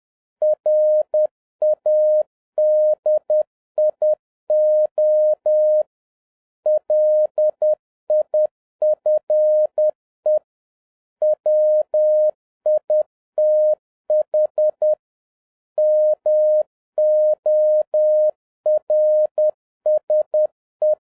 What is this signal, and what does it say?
Morse code, heard as a keyed tone, about 10 wpm, 610 Hz. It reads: RADIO LIFE WITH MORSE